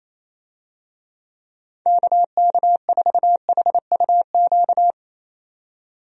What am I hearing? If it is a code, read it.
Morse code, heard as a keyed tone, about 28 wpm, 700 Hz. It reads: KK4HUQ
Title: USA Call Signs - 6 Characters - 28wpm